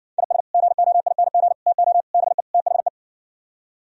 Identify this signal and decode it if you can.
Morse code, heard as a keyed tone, about 58 wpm, 700 Hz. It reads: IS GOING TO BE THE